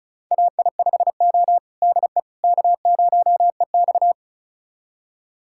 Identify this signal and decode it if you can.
Morse code, heard as a keyed tone, about 34 wpm, 700 Hz. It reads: AI5O DE K0EX
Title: USA Callsign DE Callsign - 4 Characters - 35wpm